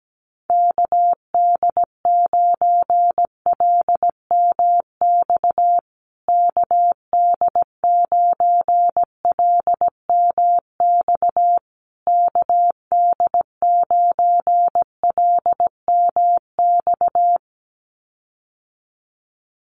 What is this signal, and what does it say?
Morse code, heard as a keyed tone, about 17 wpm, 700 Hz. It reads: KD9LMX KD9LMX KD9LMX
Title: USA Call Signs Repeated 3x 17wpm